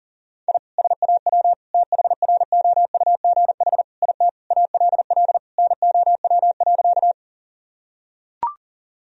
Morse code, heard as a keyed tone, about 40 wpm, 700 Hz. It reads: I SAW THROUGH IT ALL NOW.
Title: Sentences from Top 1000 Words - Review Part 1 - 40wpm